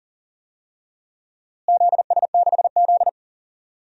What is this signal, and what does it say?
Morse code, heard as a keyed tone, about 37 wpm, 700 Hz. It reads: ZS6Z
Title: DXCC-100 Call Signs - 4 Characters - Repeated 3x - 40wpm